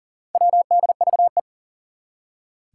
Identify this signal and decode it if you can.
Morse code, heard as a keyed tone, about 40 wpm, 700 Hz. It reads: WDVE
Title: Four Challenging Letter-Number Combinations 40wpm